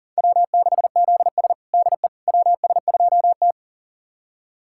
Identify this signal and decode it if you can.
Morse code, heard as a keyed tone, about 40 wpm, 700 Hz. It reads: W6ZS DE WS2T
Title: USA Callsign DE Callsign - 4 Characters - 40wpm